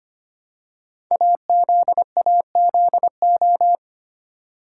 Morse code, heard as a keyed tone, about 25 wpm, 700 Hz. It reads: AZAZO
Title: Code Groups - 5 Letters - 25wpm